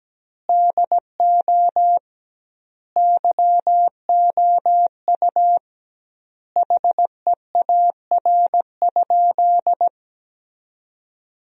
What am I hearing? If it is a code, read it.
Morse code, heard as a keyed tone, about 17 wpm, 700 Hz. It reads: DO YOU HEAR?
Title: 2x Word-Spacing - Sentences from Top 300 Words - 17wpm